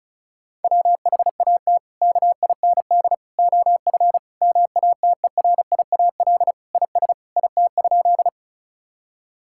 Morse code, heard as a keyed tone, about 36 wpm, 700 Hz. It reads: WHAT KIND OF MATERIAL IS IT?
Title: Sentences from Top 1000 Words - Review Part 5 - 35wpm